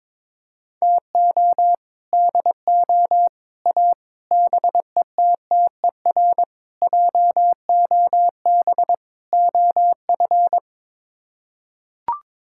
Morse code, heard as a keyed tone, about 22 wpm, 700 Hz. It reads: TO DO A BETTER JOB OF
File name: practice / 22wpm / Sets of 6 Words - 22wpm